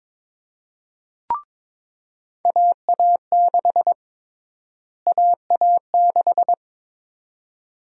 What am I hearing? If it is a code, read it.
Morse code, heard as a keyed tone, about 21 wpm, 700 Hz. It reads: AA6 AA6
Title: USA Call Sign Prefixes - Repeated 2x - 22wpm